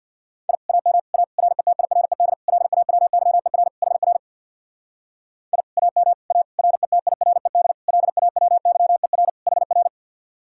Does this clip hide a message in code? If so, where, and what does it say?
Yes: Morse code from 0.5 to 9.9 s, heard as a keyed tone, about 64 wpm, 700 Hz. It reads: I AM A RETIRED LAWYER HR I AM A RETIRED LAWYER HR